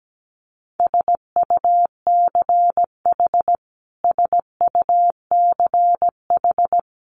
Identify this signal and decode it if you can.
Morse code, heard as a keyed tone, about 17 wpm, 700 Hz. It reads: SUCH SUCH
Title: Top 300 Words Repeated 17wpm